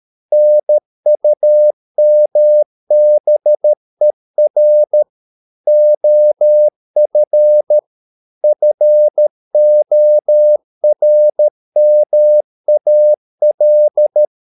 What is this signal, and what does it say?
Morse code, heard as a keyed tone, about 13 wpm, 600 Hz. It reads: NUMBER OF FORMAL